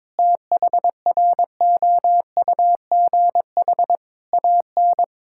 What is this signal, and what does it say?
Morse code, heard as a keyed tone, about 22 wpm, 700 Hz. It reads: THROUGH AN
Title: Sentences from Top 1600 Words 22wpm